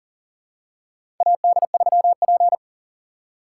Morse code, heard as a keyed tone, about 40 wpm, 700 Hz. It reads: AD3P
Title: USA Call Signs - 2x Character Spacing - 40wpm